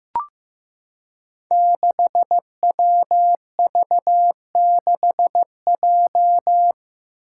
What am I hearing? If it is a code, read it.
Morse code, heard as a keyed tone, about 15 wpm, 700 Hz. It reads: 6WV6J
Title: Five Challenging Letter-Number Combinations 15wpm